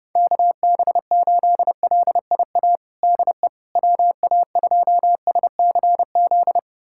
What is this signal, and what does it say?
Morse code, heard as a keyed tone, about 30 wpm, 700 Hz. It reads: KB8LIA DE WA2SCZ
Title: USA Callsign DE Callsign - 6 Characters - 30wpm